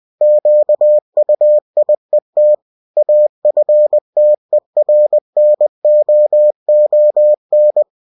Morse code, heard as a keyed tone, about 20 wpm, 600 Hz. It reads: QUIET AFTERNOON
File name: cw-books / winnie-the-pooh / winnie-20wpm-page-0066